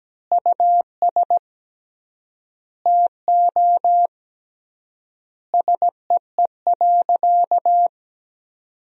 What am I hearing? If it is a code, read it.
Morse code, heard as a keyed tone, about 17 wpm, 700 Hz. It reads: US TO SEE.
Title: 3x Word-Spacing - Sentences from Top 500 Words - 17wpm